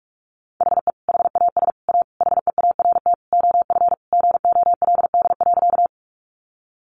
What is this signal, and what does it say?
Morse code, heard as a keyed tone, about 44 wpm, 700 Hz. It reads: HE HAS A HEART OF GOLD.
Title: Sentences from Top 1200 Words 45wpm